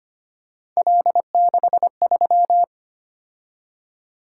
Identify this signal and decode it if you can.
Morse code, heard as a keyed tone, about 25 wpm, 700 Hz. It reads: L63
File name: Three Letter-Number Combinations - 25wpm